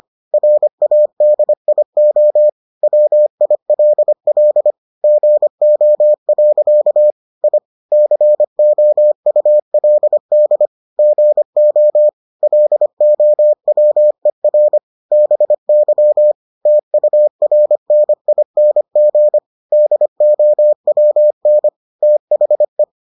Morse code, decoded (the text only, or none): RADIO WILL GO. I COULD GO LOWER BY TURNING DOWN THE